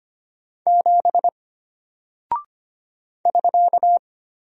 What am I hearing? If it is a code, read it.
Morse code, heard as a keyed tone, about 25 wpm, 700 Hz. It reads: 7 <SK>
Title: Single Character (Letter:Number:Punctuation:Prosign) Alt 25wpm